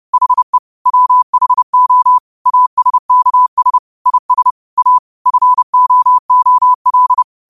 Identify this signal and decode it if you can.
Morse code, heard as a keyed tone, about 30 wpm, 1000 Hz. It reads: HE WHO ASKS IS A FOOL
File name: qotdfiles / 30wpm